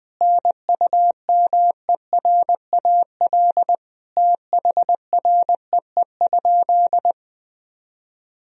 Morse code, heard as a keyed tone, about 20 wpm, 700 Hz. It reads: NUMERAL THREE?
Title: Sentences from Top 1000 Words - Review Part 2 - 20wpm